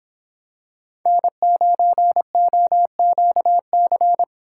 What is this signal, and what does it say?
Morse code, heard as a keyed tone, about 26 wpm, 700 Hz. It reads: N9OQC